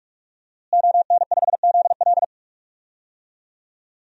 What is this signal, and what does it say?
Morse code, heard as a keyed tone, about 47 wpm, 700 Hz. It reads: ON5ZL
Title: DXCC-100 Call Signs - 5 Characters - Repeated 2x - 45wpm